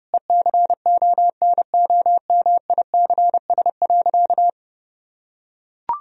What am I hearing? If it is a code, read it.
Morse code, heard as a keyed tone, about 30 wpm, 700 Hz. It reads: ECONOMICS.
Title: Sentences from Top 1000 Words - Review Part 3 - 30wpm